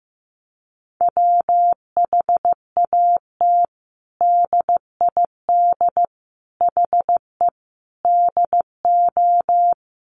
Morse code, heard as a keyed tone, about 15 wpm, 700 Hz. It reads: WHAT DID HE DO
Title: Sentences from Top 100 Words 15wpm